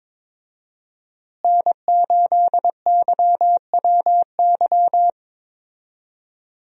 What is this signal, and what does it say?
Morse code, heard as a keyed tone, about 22 wpm, 700 Hz. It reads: N8YWY
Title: USA Call Signs - 5 Characters - Repeated 3x - 22wpm